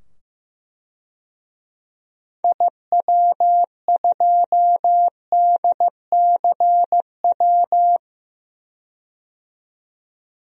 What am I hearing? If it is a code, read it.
Morse code, heard as a keyed tone, about 16 wpm, 700 Hz. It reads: IW2DCW
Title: DXCC-100 Call Signs - 6 Characters - 15wpm